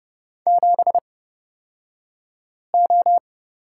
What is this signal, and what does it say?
Morse code, heard as a keyed tone, about 30 wpm, spaced as for 5 wpm, 700 Hz. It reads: 7O
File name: Lesson 36d - BT - Two Characters 30wpm